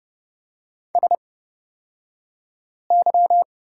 Morse code, heard as a keyed tone, about 30 wpm, spaced as for 5 wpm, 700 Hz. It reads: SY
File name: Lesson 38d - Letter q - Two Characters 30wpm